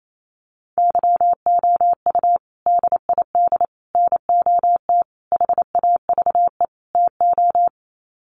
Morse code, heard as a keyed tone, about 28 wpm, 700 Hz. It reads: YOU DID NOT HAVE TO
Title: Sets of 5 Words Encore 28wpm